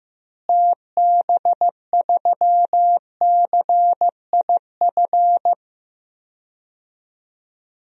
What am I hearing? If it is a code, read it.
Morse code, heard as a keyed tone, about 15 wpm, 700 Hz. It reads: TB3CIF
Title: DXCC-100 Call Signs - 6 Characters - Repeated 2x - 15wpm